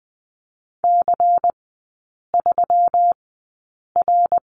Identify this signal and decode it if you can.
Morse code, heard as a keyed tone, about 20 wpm, spaced as for 8 wpm, 700 Hz. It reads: C3R
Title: 121 Lesson 20e - Number 3 - Three Characters 20wpm